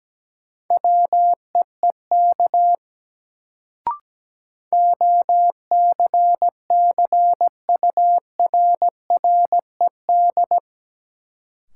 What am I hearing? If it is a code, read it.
Morse code, heard as a keyed tone, about 17 wpm, 700 Hz. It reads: WEEK OCCURRED
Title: Sets of 5 Words Encore - New Word Drill - 17wpm